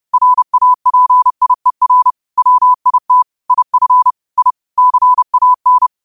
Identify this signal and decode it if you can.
Morse code, heard as a keyed tone, about 30 wpm, 1000 Hz. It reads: RAPIER WIT IF I CAN
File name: qotdfiles / 30wpm